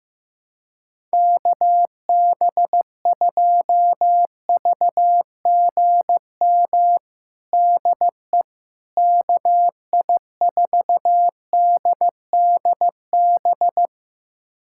Morse code, heard as a keyed tone, about 15 wpm, 700 Hz. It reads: KB2VGM DE KI4DDB
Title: USA Callsign DE Callsign - 6 Characters - 15wpm